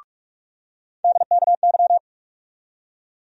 Morse code, heard as a keyed tone, about 42 wpm, 700 Hz. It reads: DKY
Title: USA Call Sign Postfixes - 3 Letters 45wpm